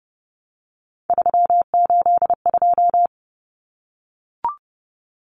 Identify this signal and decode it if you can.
Morse code, heard as a keyed tone, about 30 wpm, 700 Hz. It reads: 382
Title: Common Cut Numbers - 3-Digit 30wpm